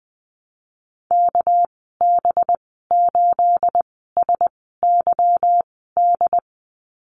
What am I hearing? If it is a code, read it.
Morse code, heard as a keyed tone, about 20 wpm, spaced as for 14 wpm, 700 Hz. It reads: KB8SYD
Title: 238 Lesson 38g - Letter q - Call Signs 20wpm